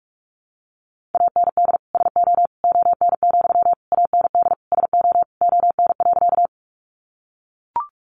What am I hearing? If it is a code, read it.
Morse code, heard as a keyed tone, about 44 wpm, 700 Hz. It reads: AND SO ON, AND SO ON.
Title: Sentences from Top 100 Words 45wpm